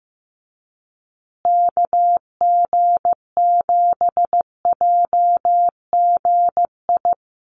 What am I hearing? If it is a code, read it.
Morse code, heard as a keyed tone, about 15 wpm, 700 Hz. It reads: KG7JGI
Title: USA Call Signs Repeated 3x 15wpm